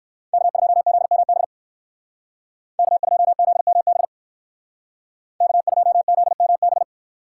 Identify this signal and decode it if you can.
Morse code, heard as a keyed tone, about 58 wpm, 700 Hz. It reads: K2ZMB K2ZMB K2ZMB